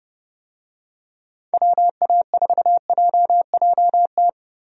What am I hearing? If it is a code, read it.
Morse code, heard as a keyed tone, about 30 wpm, 700 Hz. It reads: WA4JJT
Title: USA Call Signs - 6 Characters - Repeated 2x - 30wpm